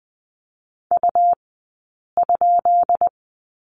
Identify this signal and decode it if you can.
Morse code, heard as a keyed tone, about 20 wpm, spaced as for 8 wpm, 700 Hz. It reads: U?
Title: 151 Lesson 25d - Letter p - Two Characters 20wpm